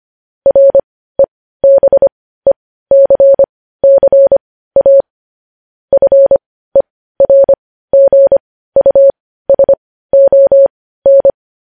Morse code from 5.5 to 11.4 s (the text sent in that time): FERGUSON